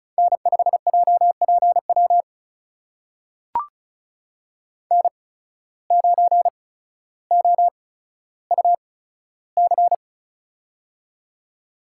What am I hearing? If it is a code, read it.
Morse code, heard as a keyed tone, about 36 wpm, spaced as for 15 wpm, 700 Hz. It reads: N5JPW N9OUC
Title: USA Call Signs - 5 Characters - 8x Character Spacing - 35wpm